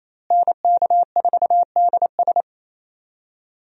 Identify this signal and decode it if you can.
Morse code, heard as a keyed tone, about 28 wpm, 700 Hz. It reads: NK4DS
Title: USA Call Signs - 5 Characters - Repeated 2x - 28wpm